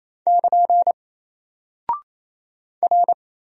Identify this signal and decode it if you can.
Morse code, heard as a keyed tone, about 27 wpm, 700 Hz. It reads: <KN> R
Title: Single Character (Letter:Number:Punctuation:Prosign) Alt 28wpm